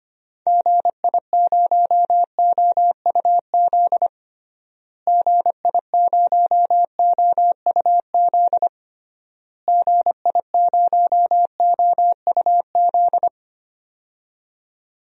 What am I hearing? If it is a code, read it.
Morse code, heard as a keyed tone, about 25 wpm, 700 Hz. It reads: GI0OUZ GI0OUZ GI0OUZ
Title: DXCC-100 Call Signs - 6 Characters - Repeated 3x - 25wpm